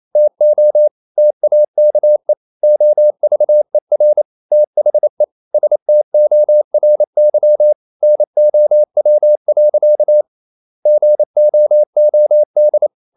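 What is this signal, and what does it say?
Morse code, heard as a keyed tone, about 28 wpm, 600 Hz. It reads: TO TAKE OVER THE STORY NOW. GOOD
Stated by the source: Morse code practice at 28 wpm, text of Ken Follett's novel Never in English, from CW med Gnister.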